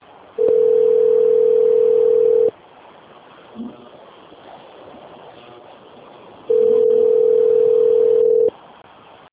Ringback tone (North America), 440 + 480 Hz.
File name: rg-1600-unknown- - - .1289